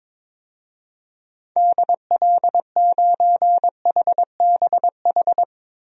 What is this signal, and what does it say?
Morse code, heard as a keyed tone, about 21 wpm, 700 Hz. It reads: DL9HBH
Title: DXCC-100 Call Signs - 6 Characters - 22wpm